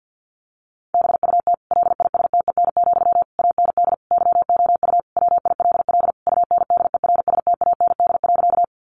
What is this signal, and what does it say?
Morse code, heard as a keyed tone, about 48 wpm, 700 Hz. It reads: BUT LISTEN, AND YOU WILL UNDERSTAND.
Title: Sentences from Top 600 Words 50wpm